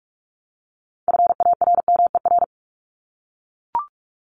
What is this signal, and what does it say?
Morse code, heard as a keyed tone, about 44 wpm, 700 Hz. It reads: FARMER
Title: Top 1751-2000 Words 45wpm